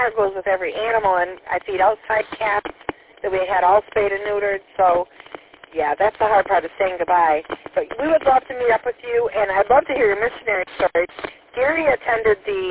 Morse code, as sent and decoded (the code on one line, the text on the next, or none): none